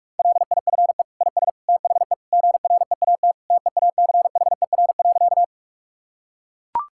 Morse code, heard as a keyed tone, about 41 wpm, 700 Hz. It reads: LIFE IS THE GREAT TEACHER.
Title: Sentences from Top 1000 Words - Review Part 6 - 45wpm